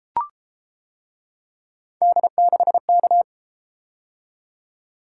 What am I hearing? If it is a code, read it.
Morse code, heard as a keyed tone, about 31 wpm, 700 Hz. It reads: D6K